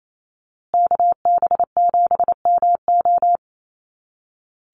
Morse code, heard as a keyed tone, about 28 wpm, 700 Hz. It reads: KB7MO
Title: USA Call Signs Repeated 2x 28wpm